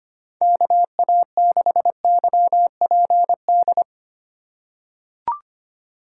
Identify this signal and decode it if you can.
Morse code, heard as a keyed tone, about 25 wpm, 700 Hz. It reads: KA6YPD
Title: USA Call Signs - 6 Characters - 2x Character Spacing - 25wpm